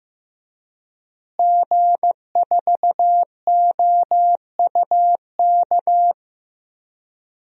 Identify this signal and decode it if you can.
Morse code, heard as a keyed tone, about 15 wpm, 700 Hz. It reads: G4OUK
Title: DXCC-100 Call Signs - 5 Characters - Repeated 3x - 15wpm